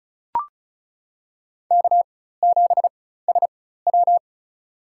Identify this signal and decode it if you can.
Morse code, heard as a keyed tone, about 34 wpm, spaced as for 16 wpm, 700 Hz. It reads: K7SW
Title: USA Call Signs - 4 Characters - 4x Character Spacing - 35wpm